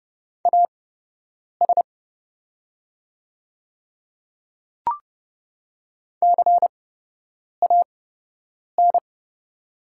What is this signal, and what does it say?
Morse code, heard as a keyed tone, about 30 wpm, spaced as for 8 wpm, 700 Hz. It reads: AS CAN
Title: 8x Character Spacing - Top 100 Words - 30wpm